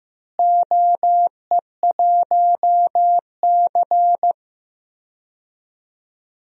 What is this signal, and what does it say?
Morse code, heard as a keyed tone, about 15 wpm, 700 Hz. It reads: OE1C
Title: DXCC-100 Call Signs - 4 Characters - Repeated 2x - 15wpm